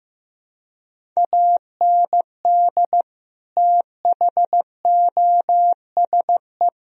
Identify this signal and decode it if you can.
Morse code, heard as a keyed tone, about 15 wpm, 700 Hz. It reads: AND THOSE
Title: Sentences from Top 1000 Words - Review Part 1 - 15wpm